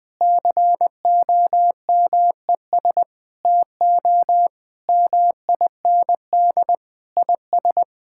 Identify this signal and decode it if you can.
Morse code, heard as a keyed tone, about 20 wpm, 700 Hz. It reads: COMES TO MIND IS
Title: Sentences with 7-Word Phrases - 20wpm